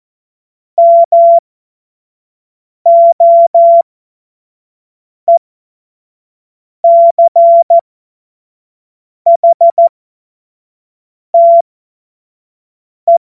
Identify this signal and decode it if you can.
Morse code, heard as a keyed tone, about 14 wpm, spaced as for 5 wpm, 680 Hz. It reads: MOECHTE
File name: Morsekurs / Witze / Witz 21. März 2023 25 F70